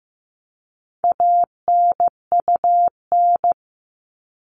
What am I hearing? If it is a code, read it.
Morse code, heard as a keyed tone, about 15 wpm, 700 Hz. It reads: ANUN
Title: All Cut Numbers - 4-Digit 15wpm